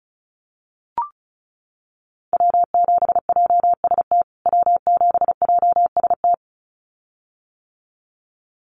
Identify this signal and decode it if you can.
Morse code, heard as a keyed tone, about 34 wpm, 700 Hz. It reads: W7JST W7JST
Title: USA Call Signs Repeated 2x 35wpm